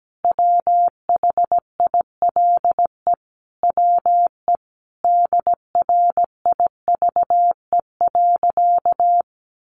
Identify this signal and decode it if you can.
Morse code, heard as a keyed tone, about 17 wpm, 700 Hz. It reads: WHILE WE DRIVE.
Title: Sentences from Top 1300 Words 17wpm